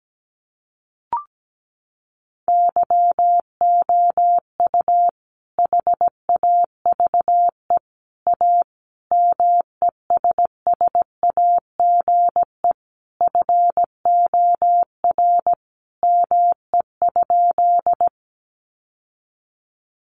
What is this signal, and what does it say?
Morse code, heard as a keyed tone, about 17 wpm, 700 Hz. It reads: YOU HAVE A MESSAGE FOR ME?
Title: Sentences from Top 1100 Words 17wpm